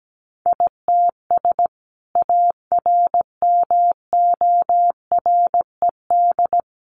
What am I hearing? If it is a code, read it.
Morse code, heard as a keyed tone, about 17 wpm, 700 Hz. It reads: ITS ARMORED
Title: A Sound Of Thunder 17wpm